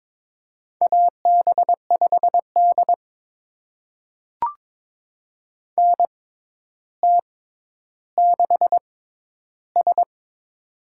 Morse code, heard as a keyed tone, about 22 wpm, spaced as for 11 wpm, 700 Hz. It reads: AB5D NT6S
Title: USA Call Signs - 4 Characters - 6x Character Spacing - 22wpm